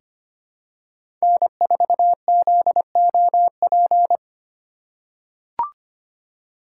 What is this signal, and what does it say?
Morse code, heard as a keyed tone, about 25 wpm, 700 Hz. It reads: N4ZOP
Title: USA Call Signs - 5 Characters - 3x Character Spacing - 25wpm